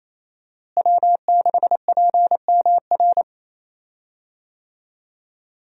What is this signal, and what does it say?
Morse code, heard as a keyed tone, about 27 wpm, 700 Hz. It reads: W6PMR
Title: USA Call Signs - 5 Characters - Repeated 3x - 28wpm